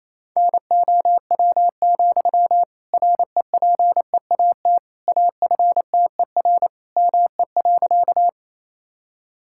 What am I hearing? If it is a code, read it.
Morse code, heard as a keyed tone, about 28 wpm, 700 Hz. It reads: NOW, REPEAT AFTER ME.
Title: Sentences from Top 1600 Words 28wpm